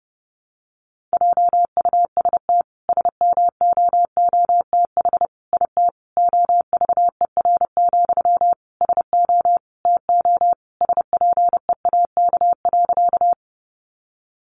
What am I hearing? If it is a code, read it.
Morse code, heard as a keyed tone, about 30 wpm, 700 Hz. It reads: JUST SMOOTH IT OVER, SO TO SPEAK.
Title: Sentences from Top 1400 Words 30wpm